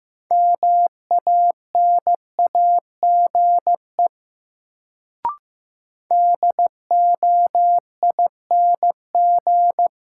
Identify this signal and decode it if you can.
Morse code, heard as a keyed tone, about 15 wpm, 700 Hz. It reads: MANAGE DOING
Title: Sentences from Top 1400 Words - New Word Drill - 15wpm